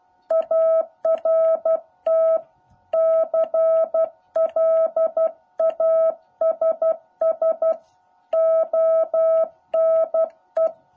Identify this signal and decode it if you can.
Morse code, heard as a keyed tone, about 12 wpm, 650 Hz. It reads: ART CLASS ONE
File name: jh8cbh / 2021cw